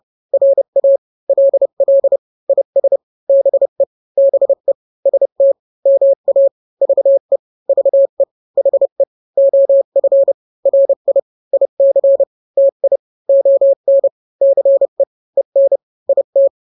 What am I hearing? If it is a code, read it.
Morse code, heard as a keyed tone, about 30 wpm, 550 Hz. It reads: RA LL IS BE BE ST MA VE VE HE OF RI IC TI ON CE EN IT
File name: n-grams / 2-grams 30 25 WPM